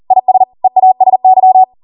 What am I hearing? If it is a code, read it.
Morse code, heard as a keyed tone, about 40 wpm, 750 Hz. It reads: IS EASY
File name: Morse / 40WPM